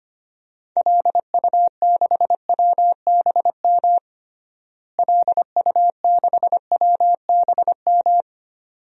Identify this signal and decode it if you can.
Morse code, heard as a keyed tone, about 25 wpm, 700 Hz. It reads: LU6WBM LU6WBM
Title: DXCC-100 Call Signs - 6 Characters - Repeated 2x - 25wpm